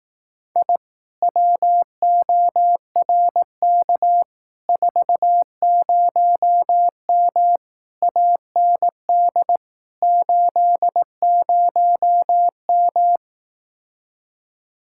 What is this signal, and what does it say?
Morse code, heard as a keyed tone, about 18 wpm, 700 Hz. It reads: I WORK 40M AND 80M
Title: Speed-Racing Common QSO Phrases Encore - 15wpm